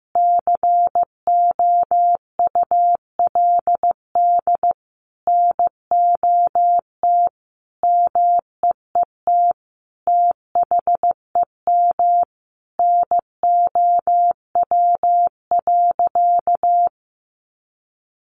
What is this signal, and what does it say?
Morse code, heard as a keyed tone, about 15 wpm, 700 Hz. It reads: COULD NOT MEET THEM NOW.
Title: Sentences from Top 1100 Words 15wpm